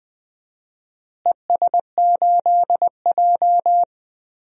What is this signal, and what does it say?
Morse code, heard as a keyed tone, about 20 wpm, 700 Hz. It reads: ES8J
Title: DXCC-100 Call Signs - 4 Characters - Repeated 2x - 20wpm